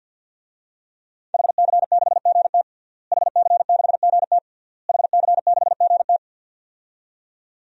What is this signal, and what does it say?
Morse code, heard as a keyed tone, about 51 wpm, 700 Hz. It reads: HC6GT HC6GT HC6GT
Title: DXCC-100 Call Signs - 5 Characters - Repeated 3x - 50wpm